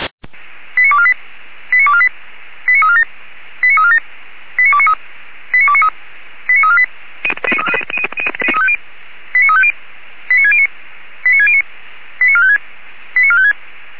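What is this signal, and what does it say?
Five-tone selective call (ZVEI), heard as quick runs of five tones, tones of about 70 ms. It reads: 89248 89248 89347 89347 89293 89293 89348 89350 89609 89609 89456 89456
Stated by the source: ZVEI five-tone selective call 89460, received on SDR.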